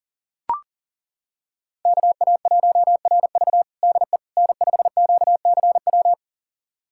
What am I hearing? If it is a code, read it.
Morse code, heard as a keyed tone, about 40 wpm, 700 Hz. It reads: KA1RV DE N5QCW
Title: USA Callsign DE Callsign - 5 Characters - 40wpm